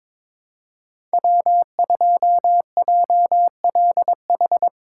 Morse code, heard as a keyed tone, about 22 wpm, 700 Hz. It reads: W2JLH
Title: USA Call Signs - 5 Characters - 3x Character Spacing - 22wpm